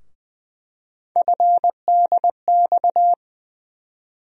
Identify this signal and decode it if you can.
Morse code, heard as a keyed tone, about 20 wpm, 700 Hz. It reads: FDX
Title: USA Call Sign Postfixes - 3 Letters 20wpm